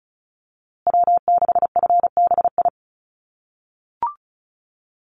Morse code, heard as a keyed tone, about 34 wpm, 700 Hz. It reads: W6FBI
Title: USA Call Signs Repeated 2x 35wpm